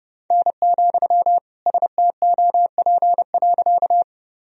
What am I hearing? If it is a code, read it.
Morse code, heard as a keyed tone, about 32 wpm, 700 Hz. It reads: N, STOP.
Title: Sentences from Top 300 Words and Abbreviations 30wpm